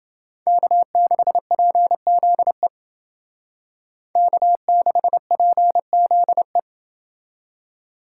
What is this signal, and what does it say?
Morse code, heard as a keyed tone, about 29 wpm, 700 Hz. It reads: K6PZE K6PZE